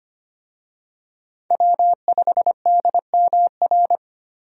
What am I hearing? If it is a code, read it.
Morse code, heard as a keyed tone, about 25 wpm, 700 Hz. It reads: W5DMR
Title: USA Call Signs - 5 Characters - 25wpm